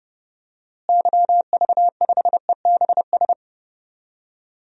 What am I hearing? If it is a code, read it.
Morse code, heard as a keyed tone, about 30 wpm, 700 Hz. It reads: YV5EBS
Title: DXCC-100 Call Signs - 6 Characters - Repeated 2x - 30wpm